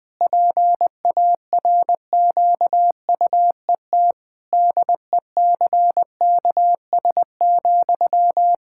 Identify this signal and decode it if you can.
Morse code, heard as a keyed tone, about 20 wpm, 700 Hz. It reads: PARQUET DECKS,
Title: Pangram Challenge v2 - 20wpm